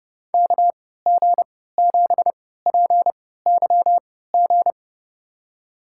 Morse code, heard as a keyed tone, about 30 wpm, spaced as for 17 wpm, 700 Hz. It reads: KG7PYG
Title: USA Call Signs - 3x Character Spacing - 30wpm